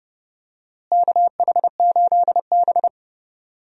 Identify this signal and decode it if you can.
Morse code, heard as a keyed tone, about 31 wpm, 700 Hz. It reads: KH8B
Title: USA Call Signs - 4 Characters - Repeated 2x - 30wpm